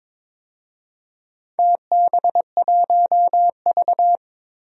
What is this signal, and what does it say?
Morse code, heard as a keyed tone, about 22 wpm, 700 Hz. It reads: TB1V
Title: DXCC-100 Call Signs - 4 Characters - Repeated 2x - 22wpm